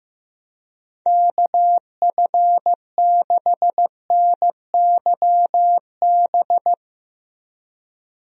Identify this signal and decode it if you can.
Morse code, heard as a keyed tone, about 15 wpm, 700 Hz. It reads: KF6NYB
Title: USA Call Signs - 6 Characters - Repeated 3x - 15wpm